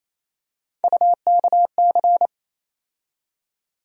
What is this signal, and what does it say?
Morse code, heard as a keyed tone, about 28 wpm, 700 Hz. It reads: UKC
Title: USA Call Sign Postfixes - 3 Letters 28wpm